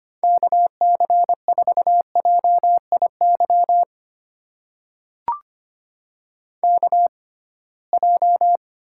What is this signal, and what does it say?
Morse code, heard as a keyed tone, about 25 wpm, spaced as for 19 wpm, 700 Hz. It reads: KC4JIY KJ
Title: USA Call Signs - 6 Characters - 6x Character Spacing - 25wpm